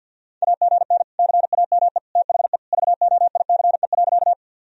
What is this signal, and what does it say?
Morse code, heard as a keyed tone, about 52 wpm, 700 Hz. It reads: AGN CAME THE VOICE.
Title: Sentences from Top 500 Words and Abbreviations 50wpm